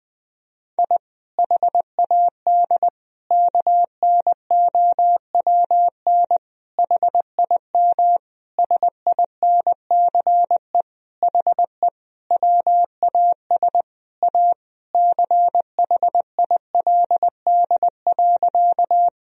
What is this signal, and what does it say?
Morse code, heard as a keyed tone, about 21 wpm, 700 Hz. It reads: I HAD KNOWN HIM SINCE HE WAS A CHILD.
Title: Sentences from Top 1000 Words - Review Part 3 - 20wpm